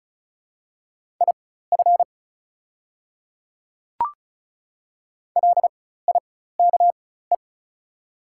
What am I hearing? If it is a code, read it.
Morse code, heard as a keyed tone, about 36 wpm, spaced as for 16 wpm, 700 Hz. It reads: IF LIKE
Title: 4x Character Spacing - Top 100 Words - 35wpm